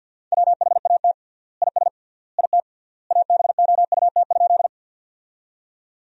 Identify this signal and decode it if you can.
Morse code, heard as a keyed tone, about 50 wpm, 700 Hz. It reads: WHAT IS IT ABOUT?
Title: 3x Word-Spacing - Sentences from Top 100 Words - 50wpm